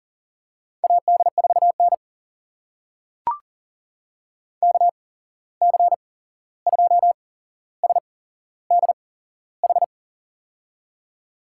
Morse code, heard as a keyed tone, about 40 wpm, spaced as for 15 wpm, 700 Hz. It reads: AD4N KC2SDH
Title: USA Call Signs - 8x Character Spacing - 40wpm